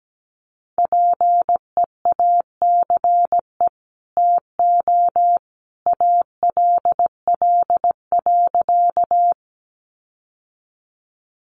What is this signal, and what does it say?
Morse code, heard as a keyed tone, about 17 wpm, 700 Hz. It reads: PEACE TO ALL.
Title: Sentences from Top 1500 Words 17wpm